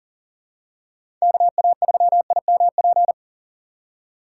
Morse code, heard as a keyed tone, about 40 wpm, 700 Hz. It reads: KA3IMP
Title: USA Call Signs - 6 Characters - 6x Character Spacing - 40wpm